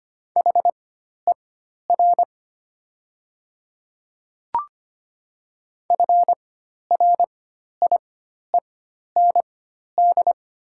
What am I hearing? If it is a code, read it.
Morse code, heard as a keyed tone, about 25 wpm, spaced as for 12 wpm, 690 Hz. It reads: HER FRIEND
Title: 4x Character Spacing - Top 300 Words - 25wpm